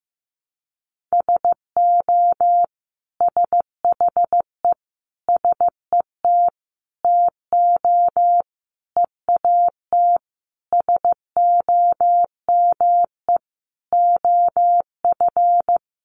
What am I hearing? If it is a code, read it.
Morse code, heard as a keyed tone, about 15 wpm, 700 Hz. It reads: SO SHE SET TO EAT SOME OF
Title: Sentences from Top 900 Words 15wpm